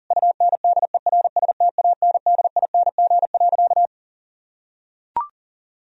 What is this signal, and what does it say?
Morse code, heard as a keyed tone, about 40 wpm, 700 Hz. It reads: UNDERSTANDING.
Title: Sentences from Top 1000 Words - Review Part 3 - 40wpm